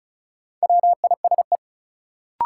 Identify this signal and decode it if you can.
Morse code, heard as a keyed tone, about 36 wpm, 700 Hz. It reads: WISE
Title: Sentences from Top 1600 Words - New Word Drill Alt - 35wpm